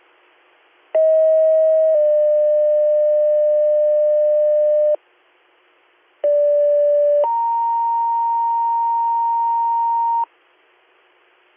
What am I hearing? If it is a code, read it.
Two-tone page, heard as a one-second tone, then a three-second tone, about 640 then 600 Hz, about 580 then 930 Hz. It is heard from 0.9 to 10.2 s.